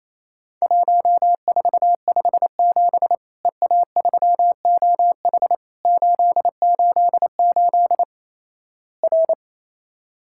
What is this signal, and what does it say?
Morse code, heard as a keyed tone, about 28 wpm, 700 Hz. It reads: 1457 EA3OH 888 R
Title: Worked All Europe DX Contest - QTC Exchange - 28wpm